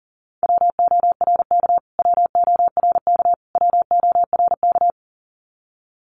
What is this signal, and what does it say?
Morse code, heard as a keyed tone, about 40 wpm, 700 Hz. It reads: WORK WORK WORK
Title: Top 500 Words Repeated 40wpm